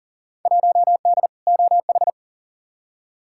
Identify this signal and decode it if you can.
Morse code, heard as a keyed tone, about 40 wpm, 700 Hz. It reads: 1D OH
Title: Field Day Exchanges - 40wpm